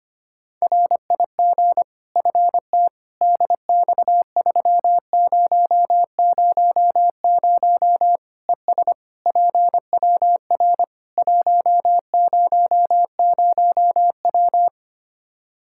Morse code, heard as a keyed tone, about 25 wpm, 700 Hz. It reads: RIG FT DX3000 ES PWR 100W
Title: Practice QSO - Exchange 2 - Info - 25wpm